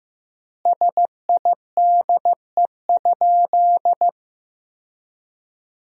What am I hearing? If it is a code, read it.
Morse code, heard as a keyed tone, about 15 wpm, 700 Hz. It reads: SIDE?
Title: 3x Word-Spacing - Sentences from Top 400 Words - 15wpm